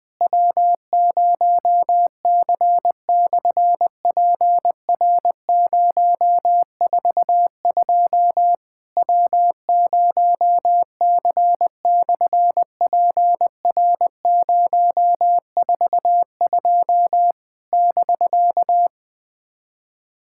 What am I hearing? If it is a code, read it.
Morse code, heard as a keyed tone, about 20 wpm, 700 Hz. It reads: W0C/PR042 W0C/PR042 <BK>
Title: SOTA QSO Practice 20wpm